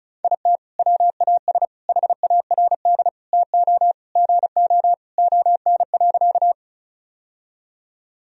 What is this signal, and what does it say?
Morse code, heard as a keyed tone, about 36 wpm, 700 Hz. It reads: IT WAS HARD TO GO ON.
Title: Sentences from Top 1000 Words - Review Part 1 - 35wpm